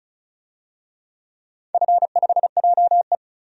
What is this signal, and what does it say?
Morse code, heard as a keyed tone, about 34 wpm, 700 Hz. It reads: F5JE
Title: DXCC-100 Call Signs - 4 Characters - 35wpm